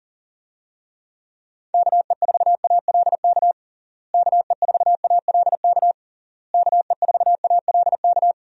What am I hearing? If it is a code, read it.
Morse code, heard as a keyed tone, about 40 wpm, 700 Hz. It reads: KE4ALK KE4ALK KE4ALK
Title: USA Call Signs - 6 Characters - Repeated 3x - 40wpm